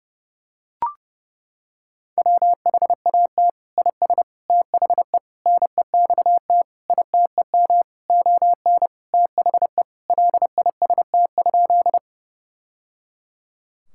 Morse code, heard as a keyed tone, about 30 wpm, 700 Hz. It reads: WHAT IS THE NEXT ITEM ON THE LIST?
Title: Sentences from Top 1000 Words - Review Part 6 - 30wpm